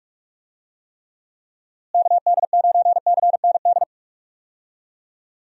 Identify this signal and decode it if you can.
Morse code, heard as a keyed tone, about 41 wpm, 700 Hz. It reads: KD9CND
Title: USA Call Signs - 6 Characters - Repeated 2x - 45wpm